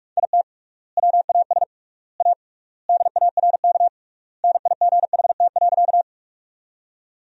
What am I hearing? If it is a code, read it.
Morse code, heard as a keyed tone, about 46 wpm, 700 Hz. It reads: IT WAS A DARK NIGHT.
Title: 3x Word-Spacing - Sentences from Top 500 Words - 45wpm